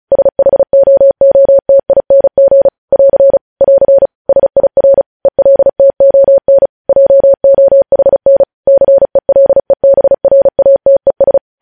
Morse code, heard as a keyed tone, about 36 wpm, 550 Hz. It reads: SHOOTING <AR> <AR> SIR ELTON JOHN CELEBRATES
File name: podcast / News headlines-1x-35x35